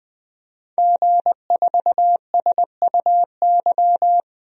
Morse code, heard as a keyed tone, about 21 wpm, 700 Hz. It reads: G4SUY
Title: DXCC-100 Call Signs - 5 Characters - 20wpm